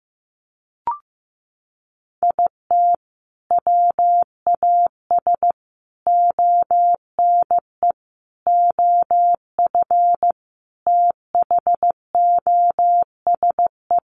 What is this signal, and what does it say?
Morse code, heard as a keyed tone, about 15 wpm, 700 Hz. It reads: IT WAS ONE OF THOSE
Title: Sets of 5 Words Encore 15wpm